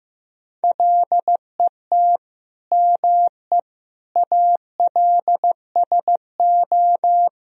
Morse code, heard as a keyed tone, about 15 wpm, 700 Hz. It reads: LET ME ALSO
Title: Sentences from Top 400 Words and Abbreviations 15wpm